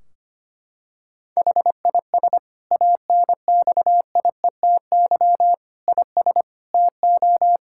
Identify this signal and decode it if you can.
Morse code, heard as a keyed tone, about 25 wpm, 700 Hz. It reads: HIS ANXIETY IS TO
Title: Sentences from Top 1600 Words 25wpm